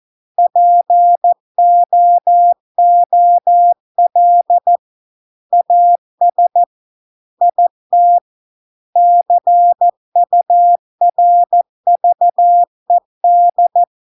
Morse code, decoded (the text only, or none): POOL AS IT CURVED